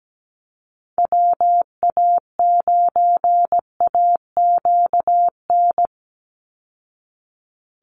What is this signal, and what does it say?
Morse code, heard as a keyed tone, about 17 wpm, 700 Hz. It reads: WA9AQN
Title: Call Signs Repeated 2x - 17wpm